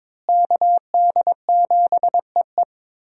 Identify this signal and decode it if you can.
Morse code, heard as a keyed tone, about 22 wpm, 700 Hz. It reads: KD7EE
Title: USA Call Signs - 5 Characters - Repeated 2x - 22wpm